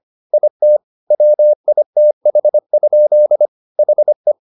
Morse code, heard as a keyed tone, about 25 wpm, 600 Hz. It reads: IT WITH? HE